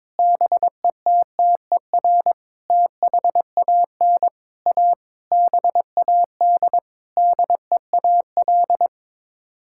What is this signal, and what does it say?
Morse code, heard as a keyed tone, about 22 wpm, 700 Hz. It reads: BETTER THAN A BAD DEAL
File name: practice / 22wpm / Sets of 8 Words - 22wpm